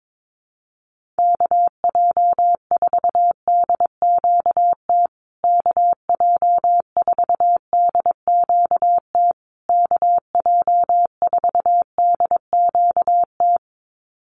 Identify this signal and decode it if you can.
Morse code, heard as a keyed tone, about 22 wpm, 700 Hz. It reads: KJ4DQT KJ4DQT KJ4DQT
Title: USA Call Signs Repeated 3x 22wpm